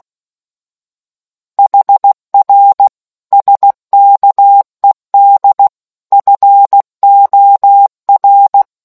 Morse code, heard as a keyed tone, about 16 wpm, 800 Hz. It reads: HR SKED FOR